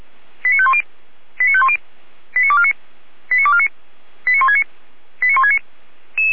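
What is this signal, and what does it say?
Five-tone selective call (ZVEI), heard as quick runs of five tones, tones of about 70 ms. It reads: 89510 89510 89249 89249 89169 89169
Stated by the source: Received on an SDR as a ZVEI five-tone call.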